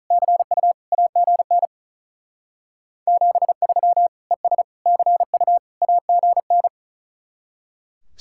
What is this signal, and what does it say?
Morse code, heard as a keyed tone, about 43 wpm, 700 Hz. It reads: CU AGN 73 ES CU AGN
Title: Speed-Racing Common QSO Phrases - 35wpm